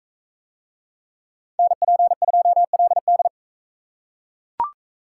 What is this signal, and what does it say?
Morse code, heard as a keyed tone, about 38 wpm, 700 Hz. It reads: NP2LD